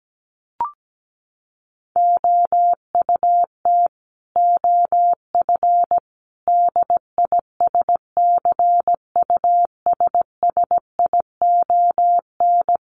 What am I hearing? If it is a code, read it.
Morse code, heard as a keyed tone, about 17 wpm, 700 Hz. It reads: OUT OF DISCUSSION
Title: Sentences from Top 1100 Words 17wpm